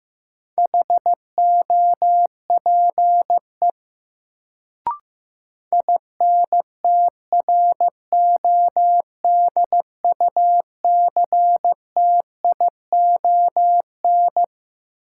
Morse code, heard as a keyed tone, about 15 wpm, 700 Hz. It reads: HOPE INTRODUCTION
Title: Sentences from Top 1000 Words - New Word Drill - 15wpm